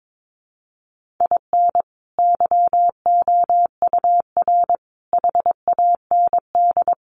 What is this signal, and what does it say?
Morse code, heard as a keyed tone, about 22 wpm, 700 Hz. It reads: IN YOUR HAND
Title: Sentences from Top 1400 Words 22wpm